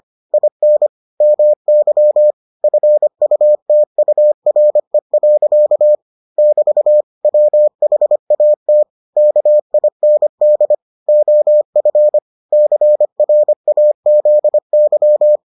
Morse code, heard as a keyed tone, about 25 wpm, 600 Hz. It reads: IN MY FUTURE. <BT> WHAT KIND OF CRAZY